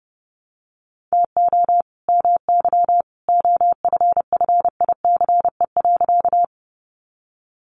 Morse code, heard as a keyed tone, about 30 wpm, 700 Hz. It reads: TO MY OFFICE.
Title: Sentences from Top 1000 Words 30wpm